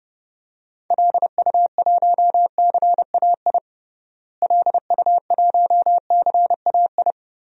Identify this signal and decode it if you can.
Morse code, heard as a keyed tone, about 32 wpm, 700 Hz. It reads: LU1CAI LU1CAI
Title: DXCC-100 Call Signs - 6 Characters - Repeated 3x - 30wpm